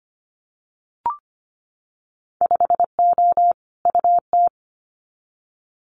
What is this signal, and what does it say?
Morse code, heard as a keyed tone, about 25 wpm, 700 Hz. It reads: 5O UT
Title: Winter Field Day Exchanges 25wpm